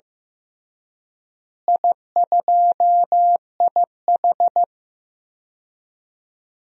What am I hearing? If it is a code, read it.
Morse code, heard as a keyed tone, about 16 wpm, 700 Hz. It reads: I2IH
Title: DXCC-100 Call Signs - 4 Characters - Repeated 2x - 15wpm